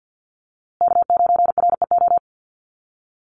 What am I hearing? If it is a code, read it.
Morse code, heard as a keyed tone, about 49 wpm, 700 Hz. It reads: K9REO
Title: USA Call Signs Repeated 3x 50wpm